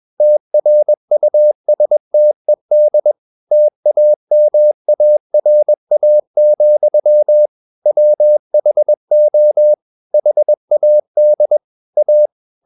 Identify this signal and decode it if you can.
Morse code, heard as a keyed tone, about 21 wpm, 600 Hz. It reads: TRUSTED TAMARA, WHO HAD A
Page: CW med Gnister